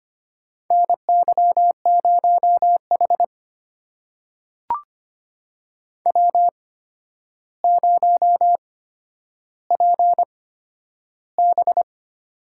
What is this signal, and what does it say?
Morse code, heard as a keyed tone, about 25 wpm, spaced as for 11 wpm, 700 Hz. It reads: NY0H W0PB
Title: USA Call Signs - 4 Characters - 8x Character Spacing - 25wpm